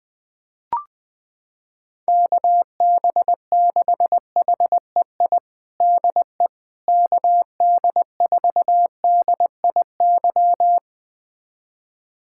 Morse code, heard as a keyed tone, about 20 wpm, 700 Hz. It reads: KB6HEI DE KD4DIY
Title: USA Callsign DE Callsign - 6 Characters - 20wpm